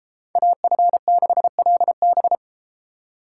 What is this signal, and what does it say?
Morse code, heard as a keyed tone, about 32 wpm, 700 Hz. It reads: AF6LB